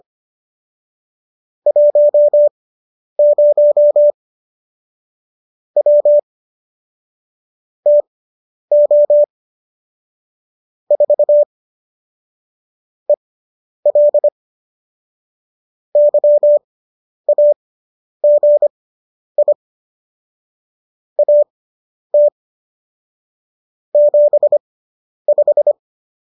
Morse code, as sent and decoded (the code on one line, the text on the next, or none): .---- ----- / .-- / - --- / ....- / . .-.. / -.-- .- --. .. / .- - / --... .....
10 W TO 4 EL YAGI AT 75